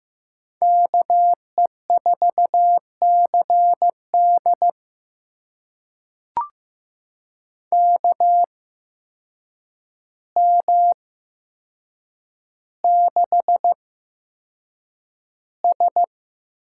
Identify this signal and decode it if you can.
Morse code, heard as a keyed tone, about 15 wpm, spaced as for 7 wpm, 700 Hz. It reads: KE4CD KM6S
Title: USA Call Signs - 5 Characters - 8x Character Spacing - 15wpm